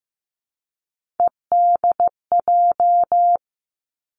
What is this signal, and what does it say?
Morse code, heard as a keyed tone, about 15 wpm, 700 Hz. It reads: EDJ
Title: Three Challenging Letter-Number Combinations 15wpm